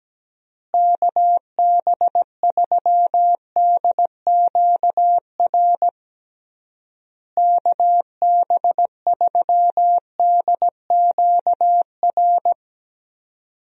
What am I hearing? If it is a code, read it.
Morse code, heard as a keyed tone, about 17 wpm, 700 Hz. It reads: KB3DQR KB3DQR
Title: USA Call Signs - 6 Characters - Repeated 3x - 17wpm